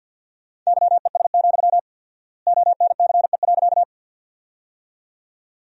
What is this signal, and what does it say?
Morse code, heard as a keyed tone, about 52 wpm, 700 Hz. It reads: YES, ONCE.
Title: 4x Word-Spacing - Sentences from Top 500 Words - 50wpm